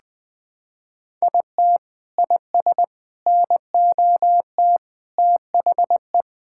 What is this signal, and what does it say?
Morse code, heard as a keyed tone, about 21 wpm, 700 Hz. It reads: IT IS NOT THE
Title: Sentences from Top 1000 Words - Review Part 5 - 20wpm